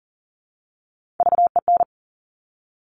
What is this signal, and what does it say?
Morse code, heard as a keyed tone, about 39 wpm, 700 Hz. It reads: VEN
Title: Top Three Letter Combinations in Words 40wpm